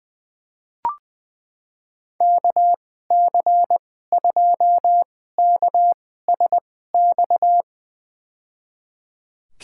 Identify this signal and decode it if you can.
Morse code, heard as a keyed tone, about 20 wpm, spaced as for 15 wpm, 700 Hz. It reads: KC2KSX
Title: USA Call Signs - 6 Characters - 2x Character Spacing - 20wpm